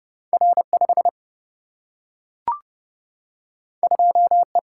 Morse code, heard as a keyed tone, about 32 wpm, 700 Hz. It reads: R 5 2 E
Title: Single-Letter Single-Number 30wpm